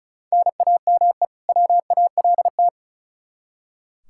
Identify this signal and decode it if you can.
Morse code, heard as a keyed tone, about 36 wpm, 700 Hz. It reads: NAME WALT
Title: Practice QSO - Exchange 1 - Beginning - 35wpm